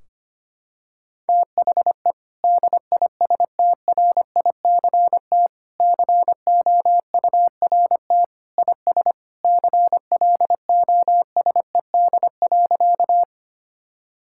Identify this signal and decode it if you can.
Morse code, heard as a keyed tone, about 25 wpm, 700 Hz. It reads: THE DISTRICT COURT IS CLOSED.
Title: Sentences from Top 1600 Words 25wpm